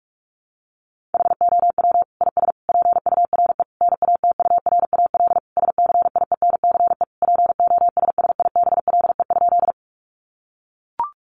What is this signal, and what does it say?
Morse code, heard as a keyed tone, about 44 wpm, 700 Hz. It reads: HOW IS PURE NATURAL SCIENCE POSSIBLE?
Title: Sentences from Top 1500 Words 45wpm